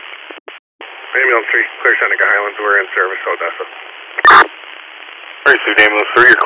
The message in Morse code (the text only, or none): none